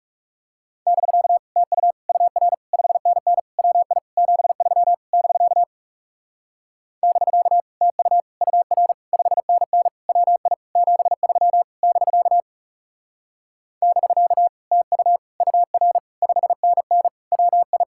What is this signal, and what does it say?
Morse code, heard as a keyed tone, about 40 wpm, 700 Hz. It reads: <BK> TU UR 5NN WI 73 <BK> <BK> TU UR 5NN WI 73 <BK> <BK> TU UR 5NN WI